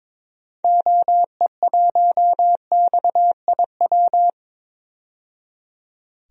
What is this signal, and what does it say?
Morse code, heard as a keyed tone, about 22 wpm, 700 Hz. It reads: OE1XIW
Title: DXCC-100 Call Signs - 6 Characters - 22wpm